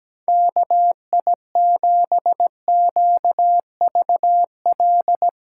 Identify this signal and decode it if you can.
Morse code, heard as a keyed tone, about 17 wpm, 700 Hz. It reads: KI7QVL
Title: USA Callsign DE Callsign 17wpm